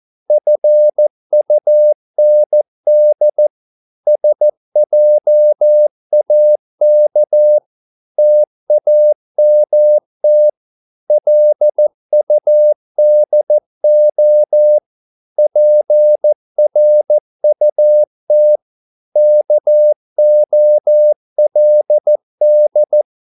Morse code, heard as a keyed tone, about 14 wpm, 600 Hz. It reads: FUND SJAK TAMT LUDO PRUT KOLD
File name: Korte ord DK 14wpm